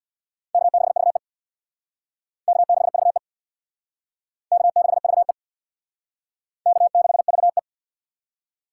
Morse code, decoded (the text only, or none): K6VE K6VE K6VE K6VE